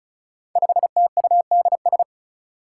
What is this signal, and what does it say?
Morse code, heard as a keyed tone, about 34 wpm, 700 Hz. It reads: 5TUDS
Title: Five Challenging Letter-Number Combinations 35wpm